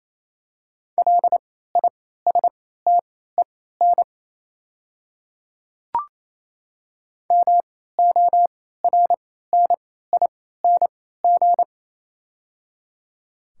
Morse code, heard as a keyed tone, about 28 wpm, spaced as for 16 wpm, 700 Hz. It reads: LISTEN MORNING
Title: 3x Character Spacing - Top 500 Words - 28wpm